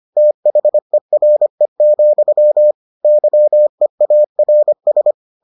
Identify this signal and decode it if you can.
Morse code, heard as a keyed tone, about 25 wpm, 600 Hz. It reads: THERE, YEARS